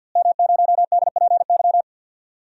Morse code, heard as a keyed tone, about 52 wpm, 700 Hz. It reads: M0DPY
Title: DXCC-100 Call Signs - 5 Characters - Repeated 3x - 50wpm